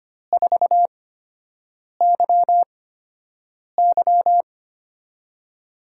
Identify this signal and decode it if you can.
Morse code, heard as a keyed tone, about 25 wpm, spaced as for 7 wpm, 700 Hz. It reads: 4YY
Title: USA Call Signs - 5 Characters - 8x Character Spacing - 25wpm